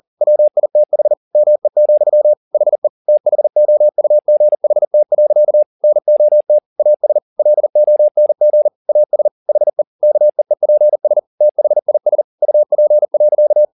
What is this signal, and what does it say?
Morse code, heard as a keyed tone, about 40 wpm, 600 Hz. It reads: WITH ME, HE THOUGHT. NOT AS LONG AS HE KEEPS THIS UP.